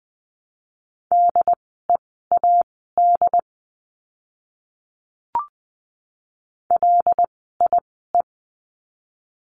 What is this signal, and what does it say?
Morse code, heard as a keyed tone, about 20 wpm, spaced as for 15 wpm, 700 Hz. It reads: DEAD LIE
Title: 075 Lesson 13f - Letter L - Words 20wpm